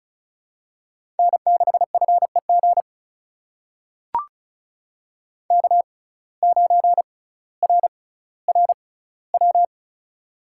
Morse code, heard as a keyed tone, about 36 wpm, spaced as for 18 wpm, 700 Hz. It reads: N6FEG K9RRW
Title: USA Call Signs - 5 Characters - 6x Character Spacing - 35wpm